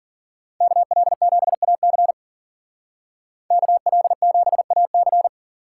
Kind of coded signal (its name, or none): Morse code